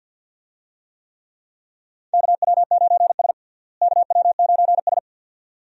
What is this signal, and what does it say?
Morse code, heard as a keyed tone, about 47 wpm, 700 Hz. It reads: KW9S KW9S
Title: USA Call Signs - 4 Characters - Repeated 2x - 50wpm